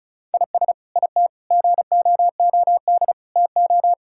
Morse code, heard as a keyed tone, about 36 wpm, 700 Hz. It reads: IS IT GOOD TO
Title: Sentences from Top 1000 Words - Review Part 6 - 35wpm